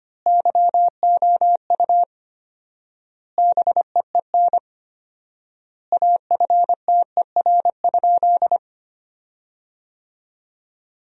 Morse code, heard as a keyed tone, about 25 wpm, 700 Hz. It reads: YOU BEEN AFTER?
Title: 4x Word-Spacing - Sentences from Top 200 Words - 25wpm